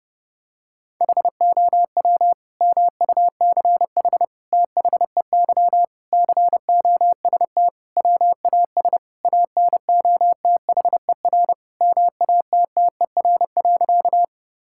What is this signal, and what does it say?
Morse code, heard as a keyed tone, about 32 wpm, 700 Hz. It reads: HOW MUCH THEY COST WAS ANOTHER MATTER.
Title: Sentences from Top 1000 Words - Review Part 5 - 30wpm